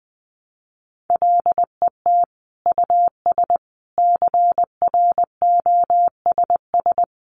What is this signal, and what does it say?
Morse code, heard as a keyed tone, about 20 wpm, 700 Hz. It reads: LET US CROSS
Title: Sentences from Top 300 Words 20wpm